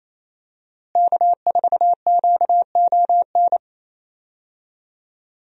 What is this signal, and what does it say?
Morse code, heard as a keyed tone, about 28 wpm, 700 Hz. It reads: K4QON
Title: USA Call Signs - 5 Characters - Repeated 3x - 28wpm